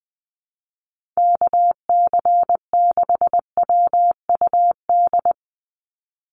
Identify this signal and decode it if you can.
Morse code, heard as a keyed tone, about 20 wpm, 700 Hz. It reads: KC6WUD
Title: USA Call Signs 20wpm